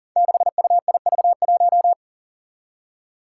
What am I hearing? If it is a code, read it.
Morse code, heard as a keyed tone, about 40 wpm, 700 Hz. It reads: 6UIV1
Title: Code Groups - 5 Letter:Numbers - 40wpm